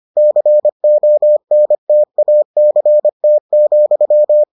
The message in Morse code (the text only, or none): CONTACT,